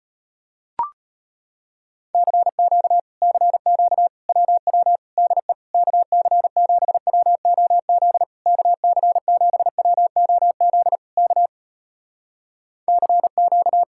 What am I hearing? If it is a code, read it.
Morse code, heard as a keyed tone, about 36 wpm, 700 Hz. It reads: CQ CQ WW DE KC7WOZ KC7WOZ K CQ